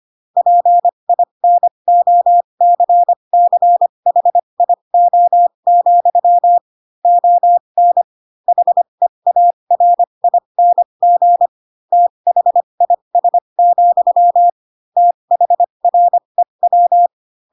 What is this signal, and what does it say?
Morse code, heard as a keyed tone, about 25 wpm, spaced as for 22 wpm, 700 Hz. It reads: PINOCCHIO, ON HEARING THIS, THREW
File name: Pinocchio / Pinocchio ch3 18 25 ik1hge